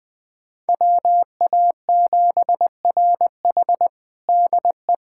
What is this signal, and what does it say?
Morse code, heard as a keyed tone, about 20 wpm, 700 Hz. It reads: WA7RH DE
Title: USA Callsign DE Callsign 20wpm